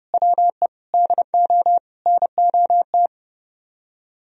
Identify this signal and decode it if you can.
Morse code, heard as a keyed tone, about 30 wpm, 700 Hz. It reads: WE DO NOT